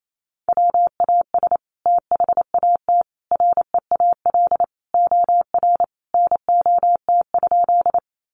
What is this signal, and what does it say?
Morse code, heard as a keyed tone, about 28 wpm, 700 Hz. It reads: WAS THAT REAL OR NOT?
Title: Sentences from Top 600 Words 28wpm